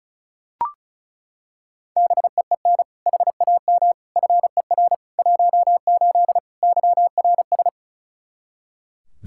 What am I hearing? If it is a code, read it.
Morse code, heard as a keyed tone, about 36 wpm, 700 Hz. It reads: BEEN HAM FER 18 YRS
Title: Practice QSO - Exchange 3 - More Info - 35wpm